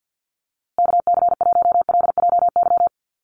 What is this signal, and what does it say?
Morse code, heard as a keyed tone, about 50 wpm, 700 Hz. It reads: KC1LJY
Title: USA Call Signs Repeated 3x 50wpm